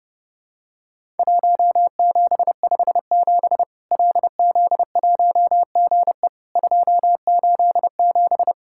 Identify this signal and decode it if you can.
Morse code, heard as a keyed tone, about 30 wpm, 700 Hz. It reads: 1757 LZ1GE 287
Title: Worked All Europe DX Contest - QTC Exchange - 30wpm